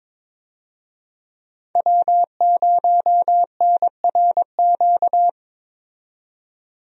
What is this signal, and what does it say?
Morse code, heard as a keyed tone, about 21 wpm, 700 Hz. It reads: W0NRQ
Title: USA Call Signs - 5 Characters - 22wpm